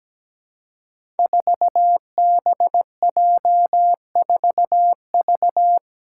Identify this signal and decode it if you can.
Morse code, heard as a keyed tone, about 17 wpm, 700 Hz. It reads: 4BJ4V
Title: Five Challenging Letter-Number Combinations 17wpm